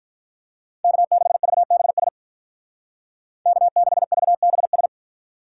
Morse code, heard as a keyed tone, about 45 wpm, 700 Hz. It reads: K6VBS K6VBS